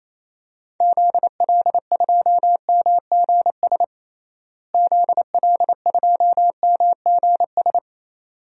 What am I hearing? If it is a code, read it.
Morse code, heard as a keyed tone, about 28 wpm, 700 Hz. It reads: ZL2MGS ZL2MGS
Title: DXCC-100 Call Signs - 6 Characters - Repeated 2x - 28wpm